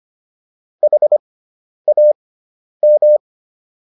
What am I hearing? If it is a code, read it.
Morse code, heard as a keyed tone, about 25 wpm, spaced as for 10 wpm, 600 Hz. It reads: HAM